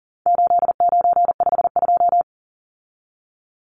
Morse code, heard as a keyed tone, about 40 wpm, 700 Hz. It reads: 8952
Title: Numbers - 4-Digit 40wpm